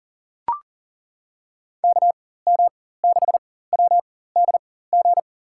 Morse code, heard as a keyed tone, about 40 wpm, spaced as for 19 wpm, 700 Hz. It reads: KM6WDG
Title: USA Call Signs - 6 Characters - 4x Character Spacing - 40wpm